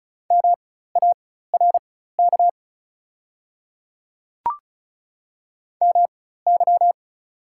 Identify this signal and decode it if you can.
Morse code, heard as a keyed tone, about 34 wpm, spaced as for 16 wpm, 700 Hz. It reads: MARK MY
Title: 4x Character Spacing - Top 500 Words - 35wpm